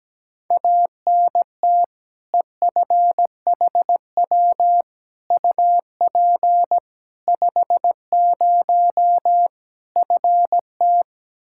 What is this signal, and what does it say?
Morse code, heard as a keyed tone, about 17 wpm, 700 Hz. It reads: ANT EFHW UP 50 FT
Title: Practice QSO - Exchange 2 - Info - 17wpm